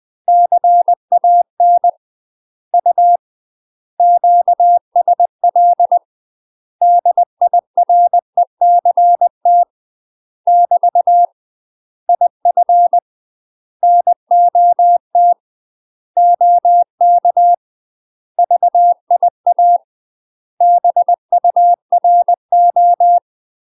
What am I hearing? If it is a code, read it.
Morse code, heard as a keyed tone, about 20 wpm, 700 Hz. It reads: CAN U QSL DIRECT <BT> IF NOT OK VIA BURO
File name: UK MASTER QSOs TRAINING / 20WPM / UK MASTER QSOs 0013